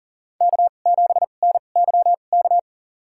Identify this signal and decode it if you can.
Morse code, heard as a keyed tone, about 37 wpm, spaced as for 28 wpm, 700 Hz. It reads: K7NYK
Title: USA Call Signs - 2x Character Spacing - 40wpm